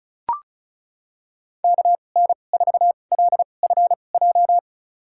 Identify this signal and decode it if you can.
Morse code, heard as a keyed tone, about 36 wpm, spaced as for 25 wpm, 700 Hz. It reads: KN4LFJ
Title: USA Call Signs - 2x Character Spacing - 35wpm